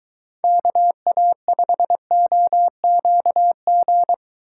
Morse code, heard as a keyed tone, about 23 wpm, 700 Hz. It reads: KA5OQG